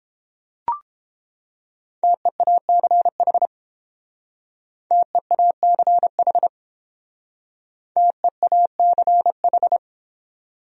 Morse code, heard as a keyed tone, about 28 wpm, 700 Hz. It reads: TEACH TEACH TEACH